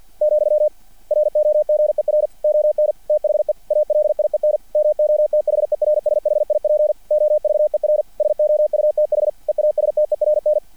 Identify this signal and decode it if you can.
Morse code, heard as a keyed tone, about 50 wpm, 600 Hz. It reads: , ROGER ON THE ALIEN MOTHERSHIP OVER SOUTH EASTERN
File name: archived / CRSnet